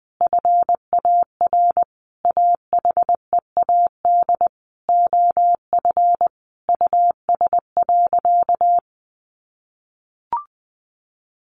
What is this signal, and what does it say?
Morse code, heard as a keyed tone, about 20 wpm, 700 Hz. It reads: FAR AHEAD OF US.
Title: Sentences from Top 1100 Words 20wpm